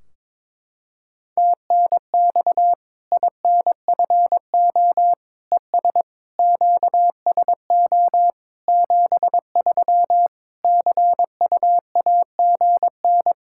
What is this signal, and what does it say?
Morse code, heard as a keyed tone, about 22 wpm, 700 Hz. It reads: TNX INFO ES QSO 73 CUAGN
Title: K1USN SST QSO Practice 22wpm